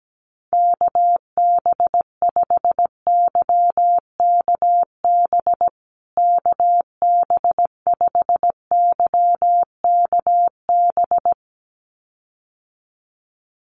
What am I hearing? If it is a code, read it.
Morse code, heard as a keyed tone, about 17 wpm, 700 Hz. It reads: KB5YKB KB5YKB
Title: USA Call Signs Repeated 2x 17wpm